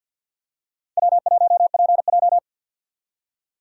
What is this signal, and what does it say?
Morse code, heard as a keyed tone, about 48 wpm, 700 Hz. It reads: W1PJ
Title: USA Call Signs - 4 Characters - 2x Character Spacing - 50wpm